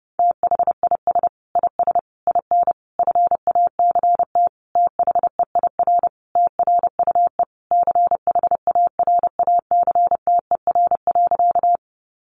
Morse code, heard as a keyed tone, about 30 wpm, 700 Hz. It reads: THIS IS IN FACT THEIR TRUE CHARACTER.
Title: Sentences from Top 1200 Words 30wpm